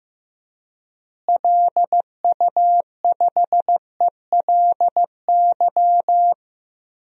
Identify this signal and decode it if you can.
Morse code, heard as a keyed tone, about 15 wpm, 700 Hz. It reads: LU5ELY
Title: DXCC-100 Call Signs - 6 Characters - Repeated 2x - 15wpm